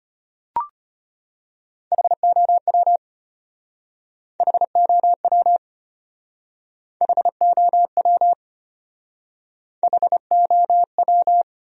Morse code, heard as a keyed tone, about 33 wpm, 700 Hz. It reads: HOW HOW HOW HOW